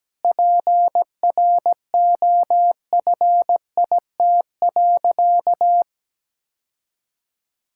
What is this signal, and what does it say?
Morse code, heard as a keyed tone, about 17 wpm, 700 Hz. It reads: PROFIT.
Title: Sentences from Top 1000 Words - Review Part 5 - 17wpm